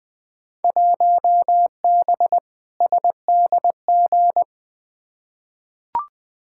Morse code, heard as a keyed tone, about 20 wpm, 700 Hz. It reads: 1B SDG
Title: Field Day Exchanges - 20wpm